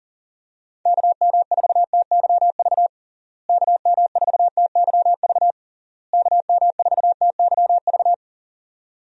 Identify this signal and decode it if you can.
Morse code, heard as a keyed tone, about 40 wpm, 700 Hz. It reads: KM4TYV KM4TYV KM4TYV
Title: USA Call Signs - 6 Characters - Repeated 3x - 40wpm